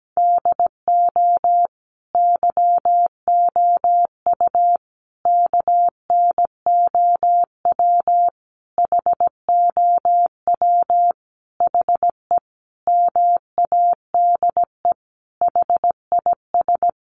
Morse code, heard as a keyed tone, about 17 wpm, 700 Hz. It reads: DO YOU KNOW HOW HE MADE HIS
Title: Sentences from Top 300 Words 17wpm